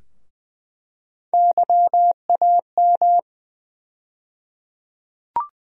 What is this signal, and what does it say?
Morse code, heard as a keyed tone, about 21 wpm, 700 Hz. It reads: YAM
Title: USA Call Sign Postfixes - 3 Letters 20wpm